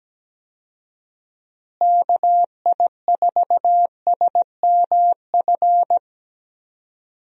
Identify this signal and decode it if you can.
Morse code, heard as a keyed tone, about 17 wpm, 700 Hz. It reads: KI4SMF
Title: USA Call Signs - 6 Characters - Repeated 3x - 17wpm